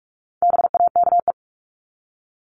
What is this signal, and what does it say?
Morse code, heard as a keyed tone, about 44 wpm, 700 Hz. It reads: BAKE
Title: Top 1751-2000 Words 45wpm